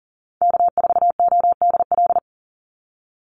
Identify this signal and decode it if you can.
Morse code, heard as a keyed tone, about 38 wpm, 700 Hz. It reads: K4ODL
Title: Call Signs - 40wpm